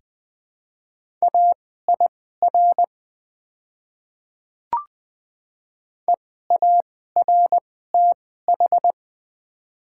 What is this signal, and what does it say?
Morse code, heard as a keyed tone, about 20 wpm, spaced as for 14 wpm, 700 Hz. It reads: AIR EARTH
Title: 2x Character Spacing - Top 200 Words - 20wpm